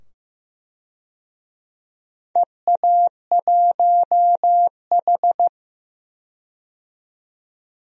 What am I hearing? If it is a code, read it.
Morse code, heard as a keyed tone, about 15 wpm, 700 Hz. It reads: EA1H
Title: DXCC-100 Call Signs - 4 Characters - 15wpm